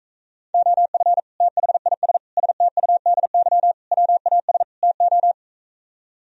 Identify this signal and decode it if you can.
Morse code, heard as a keyed tone, about 45 wpm, 700 Hz. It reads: OF THIS STUDY WAS TO